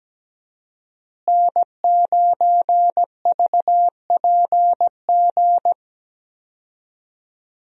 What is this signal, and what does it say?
Morse code, heard as a keyed tone, about 17 wpm, 700 Hz. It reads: N9VPG
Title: USA Call Signs - 5 Characters - Repeated 2x - 17wpm